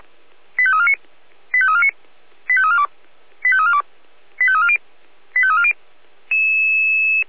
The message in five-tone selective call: 85349 85349 85342 85342 85340 85340

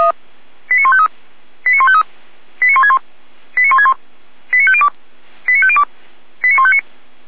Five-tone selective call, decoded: 89153 89153 89161 89161 89502 89502 89159

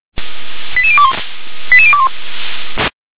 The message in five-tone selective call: 88021 88021

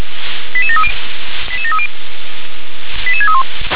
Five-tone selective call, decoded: 88730 88730 88631

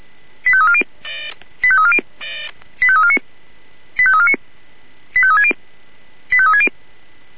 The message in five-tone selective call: 85340 85340 85349 85349 85360 85360

